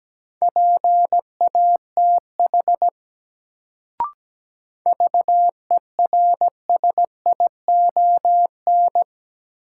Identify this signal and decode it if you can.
Morse code, heard as a keyed tone, about 17 wpm, 700 Hz. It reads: PATH VERSION
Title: Sentences from Top 1400 Words - New Word Drill - 17wpm